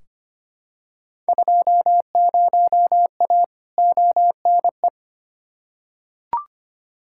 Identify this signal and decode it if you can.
Morse code, heard as a keyed tone, about 25 wpm, 700 Hz. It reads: 20A ONE
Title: Field Day Exchanges Encore - 25wpm